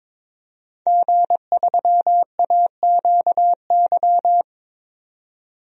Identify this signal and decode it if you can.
Morse code, heard as a keyed tone, about 22 wpm, 700 Hz. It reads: G3AQY
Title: DXCC-100 Call Signs - 5 Characters - 22wpm